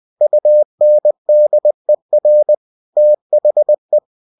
Morse code, heard as a keyed tone, about 20 wpm, 600 Hz. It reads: UNDER THE
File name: cw-books / winnie-the-pooh / winnie-20wpm-page-0107